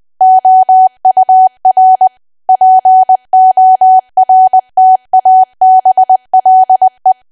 Morse code, heard as a keyed tone, about 20 wpm, 750 Hz. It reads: OUR PORTABLE
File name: Archive / 20 WPM / 20WPM